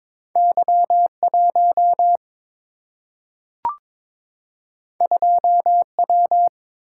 Morse code, heard as a keyed tone, about 22 wpm, 700 Hz. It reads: Y1 2W
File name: practice / 22wpm / Two Letter-Number Combinations - 22wpm